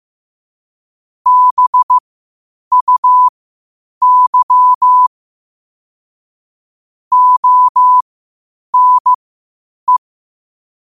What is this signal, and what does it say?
Morse code, heard as a keyed tone, about 15 wpm, spaced as for 8 wpm, 1000 Hz. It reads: BUY ONE